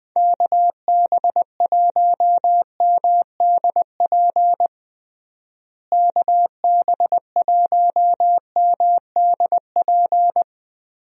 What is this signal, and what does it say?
Morse code, heard as a keyed tone, about 21 wpm, 700 Hz. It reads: KB1MDP KB1MDP
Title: USA Call Signs - 6 Characters - Repeated 2x - 20wpm